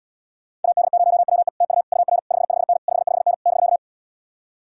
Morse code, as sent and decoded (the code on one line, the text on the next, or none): .- -.. ----- .-- . / - ..- / ..- .-. / ..... ..... -. / ..... ..... -. / -...-.-
AD0WE TU UR 55N 55N <BK>